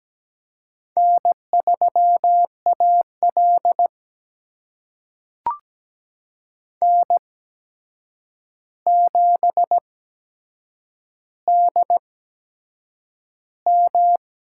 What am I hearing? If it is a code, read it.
Morse code, heard as a keyed tone, about 17 wpm, spaced as for 7 wpm, 700 Hz. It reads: N3AL N7DM
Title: USA Call Signs - 4 Characters - 8x Character Spacing - 17wpm